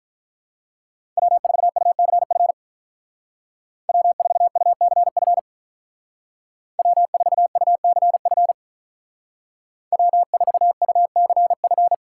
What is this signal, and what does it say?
Morse code, heard as a keyed tone, about 44 wpm, 700 Hz. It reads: W4UCF W4UCF W4UCF W4UCF
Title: Speed Racing - USA 1x3 Call Signs 35wpm